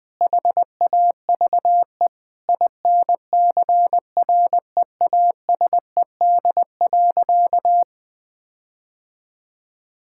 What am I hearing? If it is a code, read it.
Morse code, heard as a keyed tone, about 21 wpm, 700 Hz. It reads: HAVE INCREASED.
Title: Sentences with 6-Word Phrases - 20wpm